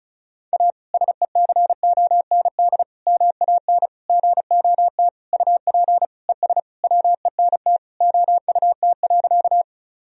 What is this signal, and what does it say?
Morse code, heard as a keyed tone, about 36 wpm, 700 Hz. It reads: A SECOND MAN GOT UP ES WENT OUT.
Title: Sentences from Top 300 Words and Abbreviations 35wpm